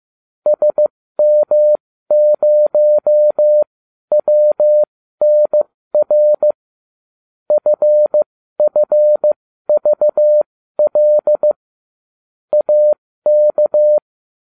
Morse code, heard as a keyed tone, about 15 wpm, spaced as for 12 wpm, 600 Hz. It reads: SM0WNR FFVL AK